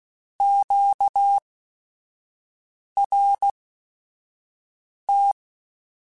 Morse code, heard as a keyed tone, about 16 wpm, spaced as for 5 wpm, 800 Hz. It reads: QRT
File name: Q Calls / Q Calls-mp3s / Q Calls Twice @5